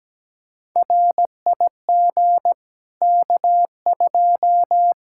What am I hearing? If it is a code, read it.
Morse code, heard as a keyed tone, about 17 wpm, 700 Hz. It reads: RIG K2
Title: Practice QSO - Exchange 2 - Info - 17wpm